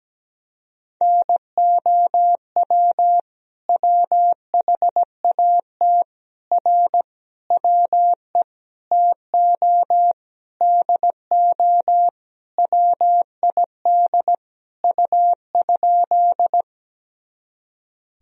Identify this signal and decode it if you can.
Morse code, heard as a keyed tone, about 17 wpm, 700 Hz. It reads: NOW WHAT R WE TO DO WID U?
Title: Sentences from Top 100 Words and Abbreviations 17wpm